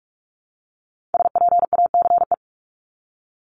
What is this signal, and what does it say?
Morse code, heard as a keyed tone, about 46 wpm, 700 Hz. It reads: SPACE
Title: Multi-letter Phonemes 45wpm